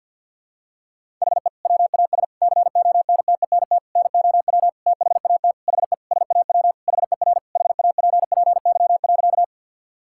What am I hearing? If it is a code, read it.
Morse code, heard as a keyed tone, about 52 wpm, 700 Hz. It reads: HE WAS CONTENT NOW THAT HE SAW HER HAPPY.
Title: Sentences from Top 1000 Words - Review Part 6 - 50wpm